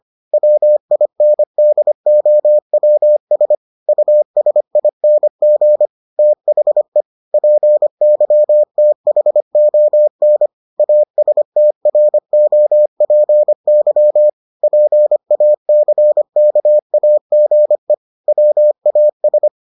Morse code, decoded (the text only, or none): WINDOWS USING THE PYTHON ASTROPY PACKAGE WAS